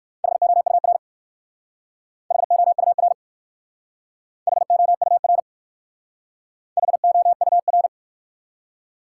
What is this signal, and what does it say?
Morse code, heard as a keyed tone, about 53 wpm, 700 Hz. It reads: HOUR HOUR HOUR HOUR
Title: Speed-Racing Top 180 QSO Elements 45wpm